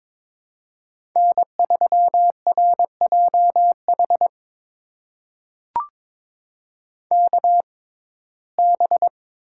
Morse code, heard as a keyed tone, about 21 wpm, spaced as for 16 wpm, 700 Hz. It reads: N3RJH KB
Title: USA Call Signs - 5 Characters - 6x Character Spacing - 22wpm